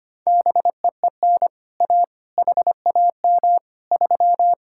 Morse code, heard as a keyed tone, about 25 wpm, 700 Hz. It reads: BEEN A HAM 3
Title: Speed-Racing Common QSO Phrases - 25wpm